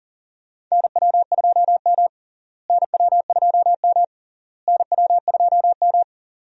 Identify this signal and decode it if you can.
Morse code, heard as a keyed tone, about 40 wpm, 700 Hz. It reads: NW2M NW2M NW2M
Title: USA Call Signs - 4 Characters - Repeated 3x - 40wpm